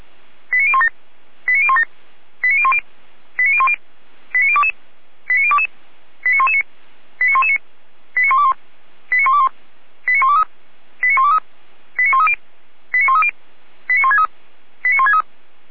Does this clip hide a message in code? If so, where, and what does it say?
Yes: five-tone selective call (ZVEI) from 0.5 to 15.2 s, heard as quick runs of five tones, tones of about 70 ms. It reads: 89017 89017 89010 89010 89022 89022 89119 89119 89121 89121 89123 89123 89130 89130 89163 89163